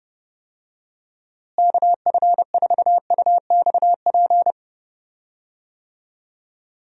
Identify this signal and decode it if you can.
Morse code, heard as a keyed tone, about 30 wpm, 700 Hz. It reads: KF4UXP
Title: USA Call Signs - 6 Characters - Repeated 3x - 30wpm